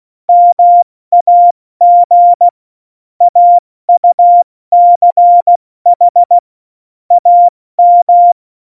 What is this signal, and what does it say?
Morse code, heard as a keyed tone, about 16 wpm, 700 Hz. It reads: MAG AUCH AM
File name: Morsekurs / Witze / Witz 13. Februar 2018, 70 F80